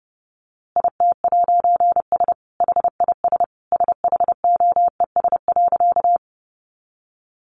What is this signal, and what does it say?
Morse code, heard as a keyed tone, about 30 wpm, 700 Hz. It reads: IT'S HIS SHOES.
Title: A Sound Of Thunder 30wpm